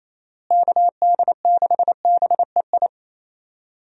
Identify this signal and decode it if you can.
Morse code, heard as a keyed tone, about 28 wpm, 700 Hz. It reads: KD6BEI
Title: USA Call Signs - 6 Characters - Repeated 3x - 28wpm